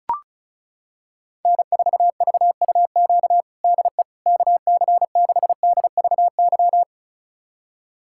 Morse code, heard as a keyed tone, about 34 wpm, 700 Hz. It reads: N4VUQ DE KC6DVY
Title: USA Callsign DE Callsign 35wpm